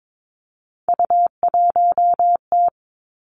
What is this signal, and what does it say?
Morse code, heard as a keyed tone, about 22 wpm, 700 Hz. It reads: U1T
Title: Three Challenging Letter-Number Combinations 22wpm